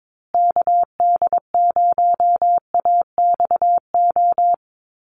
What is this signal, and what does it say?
Morse code, heard as a keyed tone, about 22 wpm, 700 Hz. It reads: KD0AXO
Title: USA Call Signs Repeated 2x 22wpm